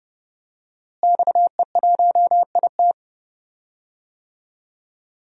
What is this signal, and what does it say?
Morse code, heard as a keyed tone, about 30 wpm, 700 Hz. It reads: XE1IT
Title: DXCC-100 Call Signs - 5 Characters - Repeated 2x - 30wpm